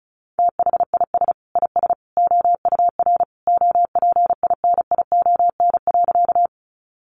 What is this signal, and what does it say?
Morse code, heard as a keyed tone, about 34 wpm, 700 Hz. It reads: THIS IS OUR OPINION.
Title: Sentences from Top 1200 Words 35wpm